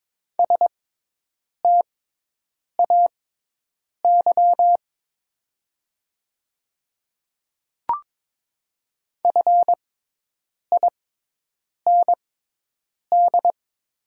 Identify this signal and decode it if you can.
Morse code, heard as a keyed tone, about 22 wpm, spaced as for 8 wpm, 700 Hz. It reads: STAY FIND
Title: 6x Character Spacing - Top 500 Words - 22wpm